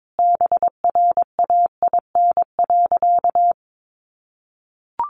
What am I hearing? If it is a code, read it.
Morse code, heard as a keyed tone, about 22 wpm, 700 Hz. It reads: BRAIN.
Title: Sentences from Top 1100 Words 22wpm